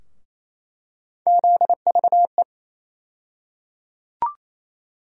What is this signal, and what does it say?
Morse code, heard as a keyed tone, about 28 wpm, 700 Hz. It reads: ZVE
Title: USA Call Sign Postfixes - 3 Letters 28wpm